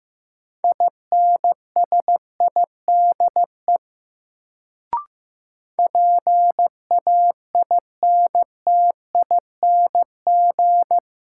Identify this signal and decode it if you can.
Morse code, heard as a keyed tone, about 15 wpm, 700 Hz. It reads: INSIDE PAINTING
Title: Sentences from Top 1200 Words - New Word Drill Alt - 15wpm